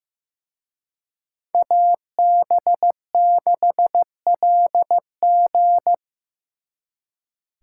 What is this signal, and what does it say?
Morse code, heard as a keyed tone, about 18 wpm, 700 Hz. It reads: AB6LG
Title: Speed Racing - USA 2x2 Call Signs 15wpm